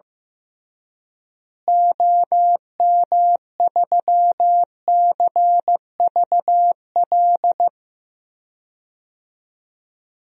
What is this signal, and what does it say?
Morse code, heard as a keyed tone, about 15 wpm, 700 Hz. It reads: OM3CVL
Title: DXCC-100 Call Signs - 6 Characters - Repeated 3x - 15wpm